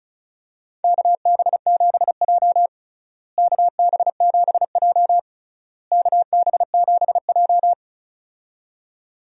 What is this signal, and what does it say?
Morse code, heard as a keyed tone, about 36 wpm, 700 Hz. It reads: KB7J KB7J KB7J
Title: USA Call Signs - 4 Characters - Repeated 3x - 35wpm